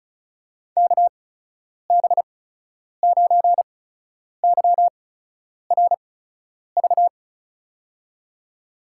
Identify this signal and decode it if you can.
Morse code, heard as a keyed tone, about 36 wpm, spaced as for 10 wpm, 700 Hz. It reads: KB9YRV
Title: USA Call Signs - 6 Characters - 8x Character Spacing - 35wpm